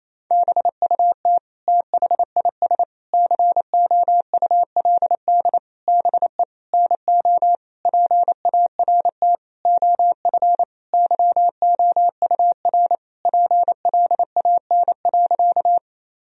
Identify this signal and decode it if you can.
Morse code, heard as a keyed tone, about 28 wpm, 700 Hz. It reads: BUT THIS COULD BE NO PART OF YOUR PLAN.
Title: Sentences from Top 1000 Words - Review Part 2 - 28wpm